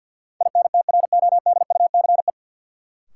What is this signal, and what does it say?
Morse code, heard as a keyed tone, about 50 wpm, 700 Hz. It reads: INTRODUCE
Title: Sentences from Top 1600 Words - New Word Drill Alt - 50wpm